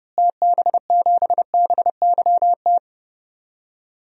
Morse code, heard as a keyed tone, about 30 wpm, 700 Hz. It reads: TB7BYT
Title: DXCC-100 Call Signs - 6 Characters - Repeated 3x - 30wpm